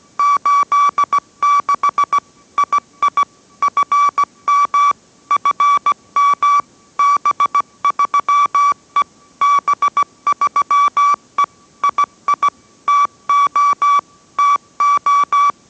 Morse code, heard as a keyed tone, about 17 wpm, 1150 Hz. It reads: 86 II FM FM B3E B3E II TO TO